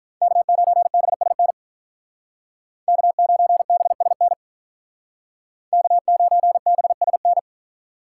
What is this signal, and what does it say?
Morse code, heard as a keyed tone, about 46 wpm, 700 Hz. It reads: K9BSN K9BSN K9BSN